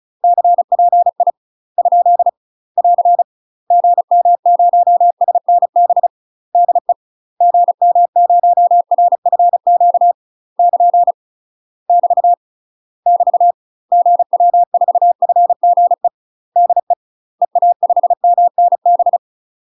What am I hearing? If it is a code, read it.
Morse code, heard as a keyed tone, about 36 wpm, 700 Hz. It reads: CPI ? <AR> GM0SNB DE GM0RFQ <KN> <BT> <BT> GW4FGE DE EA5MNB
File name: UK MASTER QSOs TRAINING / 35WPM / UK MASTER QSOs 0013